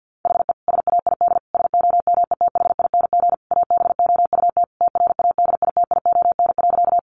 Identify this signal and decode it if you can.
Morse code, heard as a keyed tone, about 50 wpm, 700 Hz. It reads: HE SAID SOMETHING ABOUT TRADITION.
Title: Sentences from Top 1500 Words 50wpm